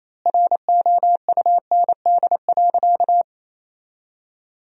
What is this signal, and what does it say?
Morse code, heard as a keyed tone, about 28 wpm, 700 Hz. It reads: ROUND.
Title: 4x Word-Spacing - Sentences from Top 200 Words - 28wpm